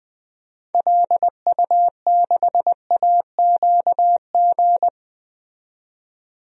Morse code, heard as a keyed tone, about 20 wpm, 700 Hz. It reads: LU6AQG
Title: DXCC-100 Call Signs - 6 Characters - 20wpm